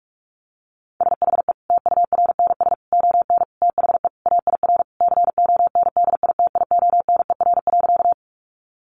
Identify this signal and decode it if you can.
Morse code, heard as a keyed tone, about 44 wpm, 700 Hz. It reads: SHE TURNS ON THE AIR CONDITIONER.
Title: Additional Easy Phrases 45wpm